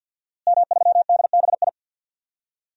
Morse code, heard as a keyed tone, about 47 wpm, 700 Hz. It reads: M3DBI
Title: DXCC-100 Call Signs - 5 Characters - Repeated 2x - 50wpm